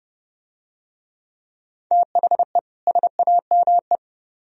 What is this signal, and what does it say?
Morse code, heard as a keyed tone, about 29 wpm, 700 Hz. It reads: THE SAME